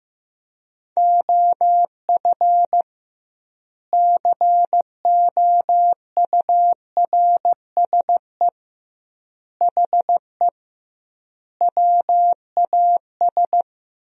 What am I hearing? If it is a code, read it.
Morse code, heard as a keyed tone, about 15 wpm, 700 Hz. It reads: OF COURSE HE WAS
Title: 2x Word-Spacing - Sentences from Top 500 Words - 15wpm